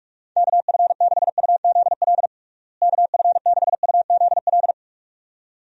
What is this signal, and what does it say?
Morse code, heard as a keyed tone, about 46 wpm, 700 Hz. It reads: KF6UZL KF6UZL
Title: USA Call Signs - 6 Characters - Repeated 2x - 45wpm